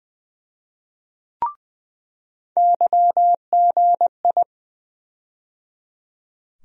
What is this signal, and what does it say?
Morse code, heard as a keyed tone, about 20 wpm, 700 Hz. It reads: YGI
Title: USA Call Sign Postfixes - 3 Letters 20wpm